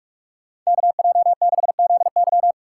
Morse code, heard as a keyed tone, about 46 wpm, 700 Hz. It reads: KJ6ZY
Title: USA Call Signs - 5 Characters - 6x Character Spacing - 45wpm